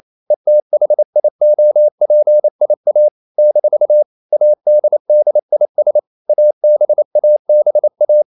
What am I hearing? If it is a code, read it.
Morse code, heard as a keyed tone, about 28 wpm, 600 Hz. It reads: ETHIOPIA - ADDIS ABABA
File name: CW / Practice / NationalCapitals28